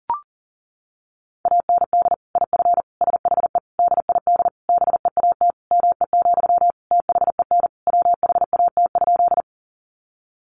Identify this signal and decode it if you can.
Morse code, heard as a keyed tone, about 40 wpm, 700 Hz. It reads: AND IF SHE DID BEAT ME, THEN WHAT?
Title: Sentences from Top 1300 Words 40wpm